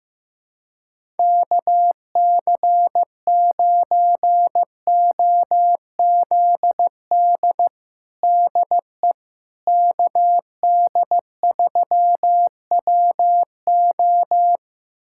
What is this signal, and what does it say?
Morse code, heard as a keyed tone, about 15 wpm, 700 Hz. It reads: KC9OZD DE KD3WO
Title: USA Callsign DE Callsign 15wpm